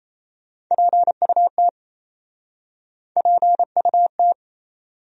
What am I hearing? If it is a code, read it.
Morse code, heard as a keyed tone, about 37 wpm, 700 Hz. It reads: PUT PUT